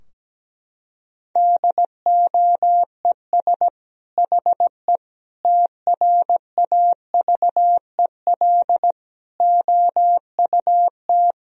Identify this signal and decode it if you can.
Morse code, heard as a keyed tone, about 17 wpm, 700 Hz. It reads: DOES HE TRAVEL OUT
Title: Sentences from Top 1000 Words - Review Part 2 - 17wpm